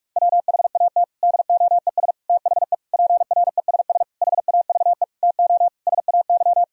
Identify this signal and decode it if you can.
Morse code, heard as a keyed tone, about 46 wpm, 700 Hz. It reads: WHAT DOES THE PRESS HAVE TO SAY
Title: Sentences from Top 1000 Words - Review Part 3 - 45wpm